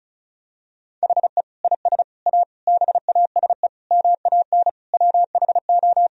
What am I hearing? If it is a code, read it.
Morse code, heard as a keyed tone, about 36 wpm, 700 Hz. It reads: HE IS A BASE MAN WHO
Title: Sentences from Top 1000 Words - Review Part 1 - 35wpm